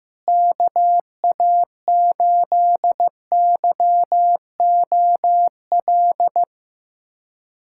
Morse code, heard as a keyed tone, about 15 wpm, 700 Hz. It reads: KA8YOL
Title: USA Callsign DE Callsign - 6 Characters - 15wpm